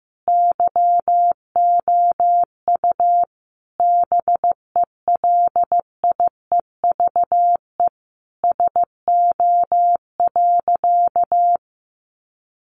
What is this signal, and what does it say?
Morse code, heard as a keyed tone, about 15 wpm, 700 Hz. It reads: YOU BELIEVE SO.
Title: Sentences from Top 1000 Words 15wpm